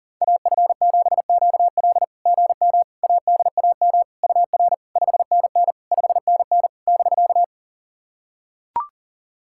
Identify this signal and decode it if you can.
Morse code, heard as a keyed tone, about 44 wpm, 690 Hz. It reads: AF7QL GM ADAM UR 5NN 5NN <BK>
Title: SOTA QSO Practice 40wpm